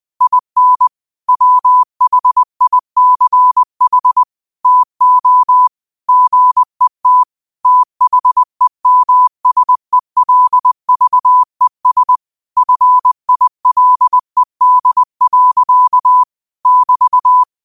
Morse code, heard as a keyed tone, about 20 wpm, 1000 Hz. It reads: IN WHICH TO GET THEMSELVES FILED. <BT>
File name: qotdfiles / 20wpm